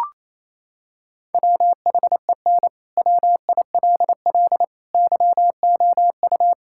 Morse code, heard as a keyed tone, about 28 wpm, 700 Hz. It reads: WHEN WILL YOU
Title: Sentences from Top 1600 Words 28wpm